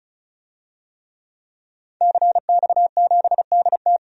Morse code, heard as a keyed tone, about 36 wpm, 700 Hz. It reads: CX7DT
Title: DXCC-100 Call Signs - 5 Characters - 35wpm